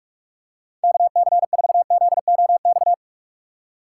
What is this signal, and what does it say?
Morse code, heard as a keyed tone, about 46 wpm, 700 Hz. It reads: KC4ZOX
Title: USA Call Signs - 6 Characters - 2x Character Spacing - 45wpm